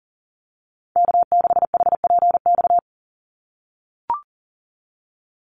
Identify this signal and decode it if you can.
Morse code, heard as a keyed tone, about 40 wpm, 700 Hz. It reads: K6HPX
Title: NA SOTA Activator Callsigns 40wpm